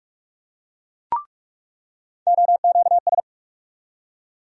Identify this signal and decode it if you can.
Morse code, heard as a keyed tone, about 47 wpm, 700 Hz. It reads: OQS
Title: USA Call Sign Postfixes - 3 Letters 45wpm